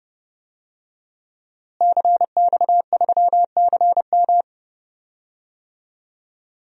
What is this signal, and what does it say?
Morse code, heard as a keyed tone, about 32 wpm, 700 Hz. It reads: CX3CM
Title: DXCC-100 Call Signs - 5 Characters - Repeated 3x - 30wpm